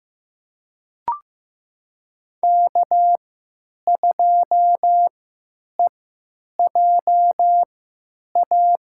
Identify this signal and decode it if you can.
Morse code, heard as a keyed tone, about 15 wpm, spaced as for 8 wpm, 700 Hz. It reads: K2EJA
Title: USA Call Signs - 5 Characters - 3x Character Spacing - 15wpm